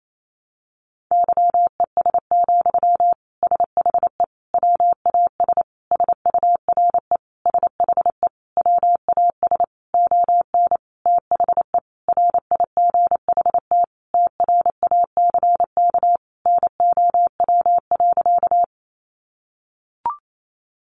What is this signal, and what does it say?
Morse code, heard as a keyed tone, about 28 wpm, 700 Hz. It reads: YES, SHE WAS SURE SHE WAS ON THE RIGHT TRACK NOW.
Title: Sentences from Top 1100 Words 28wpm